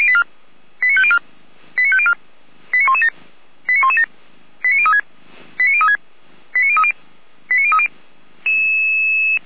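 Five-tone selective call (ZVEI), heard as quick runs of five tones, tones of about 80 ms. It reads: 89504 89504 89117 89117 89036 89036 89030 89030
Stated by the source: Five-tone ZVEI selective call received by SDR.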